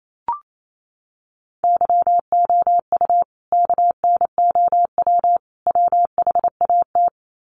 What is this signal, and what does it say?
Morse code, heard as a keyed tone, about 28 wpm, 700 Hz. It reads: YOU KNOW WHAT
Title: Sets of 3 Words 28wpm